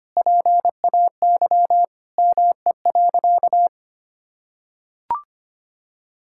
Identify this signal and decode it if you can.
Morse code, heard as a keyed tone, about 25 wpm, 700 Hz. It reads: PAY ME.
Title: Sentences from Top 1000 Words - Review Part 5 - 25wpm